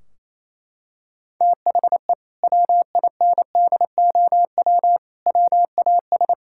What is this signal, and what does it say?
Morse code, heard as a keyed tone, about 28 wpm, 700 Hz. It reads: THE WINDOW WAS
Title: Sentences from Top 1600 Words 28wpm